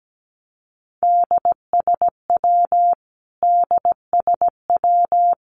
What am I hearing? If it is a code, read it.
Morse code, heard as a keyed tone, about 17 wpm, 700 Hz. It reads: DSW DSW
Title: Abbreviations Repeated 3x - 17wpm